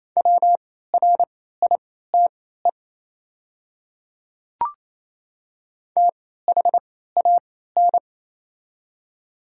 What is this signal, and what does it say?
Morse code, heard as a keyed tone, about 27 wpm, spaced as for 16 wpm, 700 Hz. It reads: WRITE THAN
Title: 3x Character Spacing - Top 100 Words - 28wpm